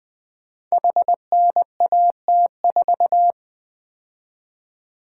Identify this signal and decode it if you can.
Morse code, heard as a keyed tone, about 20 wpm, 700 Hz. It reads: HNAT4
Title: Code Groups - 5 Letter:Numbers - 20wpm